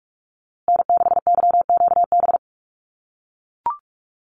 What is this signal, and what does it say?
Morse code, heard as a keyed tone, about 44 wpm, 700 Hz. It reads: N6YQB
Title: USA Call Signs Repeated 2x 45wpm